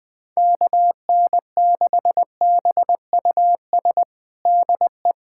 Morse code, heard as a keyed tone, about 21 wpm, 700 Hz. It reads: KN6BUS DE
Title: USA Callsign DE Callsign - 6 Characters - 20wpm